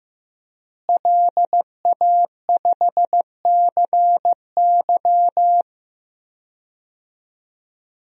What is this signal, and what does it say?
Morse code, heard as a keyed tone, about 15 wpm, 700 Hz. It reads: LA5CY
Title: DXCC-100 Call Signs - 5 Characters - Repeated 3x - 15wpm